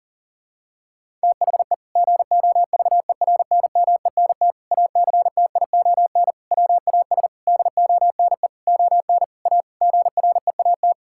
Morse code, heard as a keyed tone, about 40 wpm, 700 Hz. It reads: THE GOVERNMENT ACTION WAS DONE ON A GREAT
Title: Sentences from Top 1000 Words - Review Part 5 - 40wpm